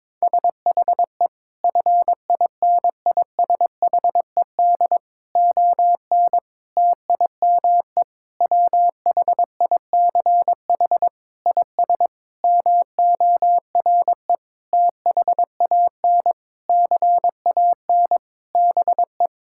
Morse code, heard as a keyed tone, about 22 wpm, 700 Hz. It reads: SHE FINISHED ON TIME WHICH IS MORE THAN CAN BE
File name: practice / 22wpm / Sentences with 8-Word Phrases - 22wpm